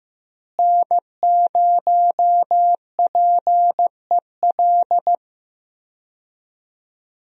Morse code, heard as a keyed tone, about 15 wpm, 700 Hz. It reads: N0PEL
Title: USA Call Signs - 5 Characters - Repeated 3x - 15wpm